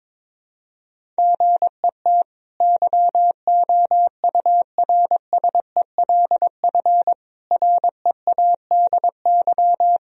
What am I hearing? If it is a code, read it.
Morse code, heard as a keyed tone, about 22 wpm, 700 Hz. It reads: GET YOURSELF READY
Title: Sentences from Top 1000 Words - Review Part 1 - 22wpm